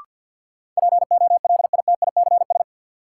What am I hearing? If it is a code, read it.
Morse code, heard as a keyed tone, about 52 wpm, 700 Hz. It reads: POLITICS
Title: Sentences from Top 1000 Words - New Word Drill - 50wpm